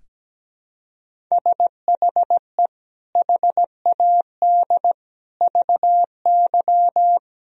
Morse code, heard as a keyed tone, about 17 wpm, 700 Hz. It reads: SHE HAD VY
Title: Sentences from Top 400 Words and Abbreviations 17wpm